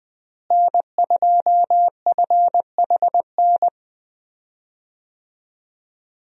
Morse code, heard as a keyed tone, about 20 wpm, 700 Hz. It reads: N2FHN
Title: USA Call Signs - 5 Characters - Repeated 2x - 20wpm